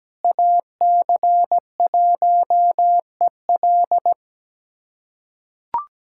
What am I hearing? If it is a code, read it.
Morse code, heard as a keyed tone, about 17 wpm, 700 Hz. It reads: AC1EL
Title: USA Callsign DE Callsign 17wpm